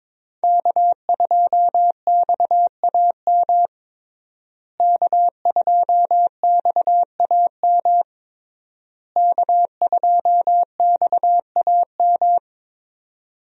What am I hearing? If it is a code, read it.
Morse code, heard as a keyed tone, about 22 wpm, 700 Hz. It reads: K2XAM K2XAM K2XAM
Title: USA Call Signs - 5 Characters - Repeated 3x - 22wpm